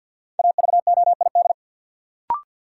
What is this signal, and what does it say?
Morse code, heard as a keyed tone, about 53 wpm, 700 Hz. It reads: AVOID
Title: Sentences from Top 1000 Words - Review Part 6 - New Word Drill - 50wpm